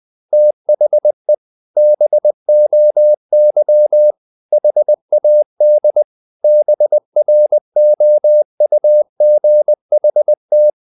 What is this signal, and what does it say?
Morse code, heard as a keyed tone, about 20 wpm, 600 Hz. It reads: THE BOY HAD BROUGHT
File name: the old man and the sea / the old man and the sea part 1 20wpm